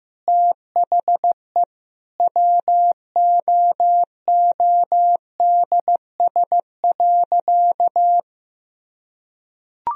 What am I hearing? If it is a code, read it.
Morse code, heard as a keyed tone, about 15 wpm, 700 Hz. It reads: THE WOODS.
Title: Sentences from Top 1000 Words - Review Part 2 - 15wpm